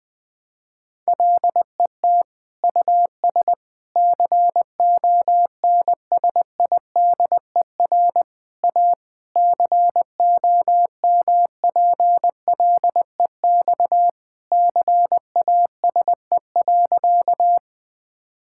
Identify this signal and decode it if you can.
Morse code, heard as a keyed tone, about 20 wpm, 700 Hz. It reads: LET US CONSIDER A COMPLEX CASE.
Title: Sentences from Top 1000 Words - Review Part 6 - 20wpm